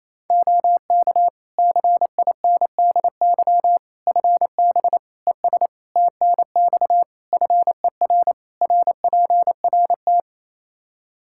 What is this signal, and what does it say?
Morse code, heard as a keyed tone, about 28 wpm, 700 Hz. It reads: OK CINDY FB ES TNX FER RPRT
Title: K1USN SST QSO Practice 28wpm